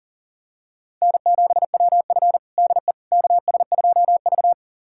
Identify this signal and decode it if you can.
Morse code, heard as a keyed tone, about 40 wpm, 700 Hz. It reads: N7WF DE KS2V
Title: USA Callsign DE Callsign - 4 Characters - 40wpm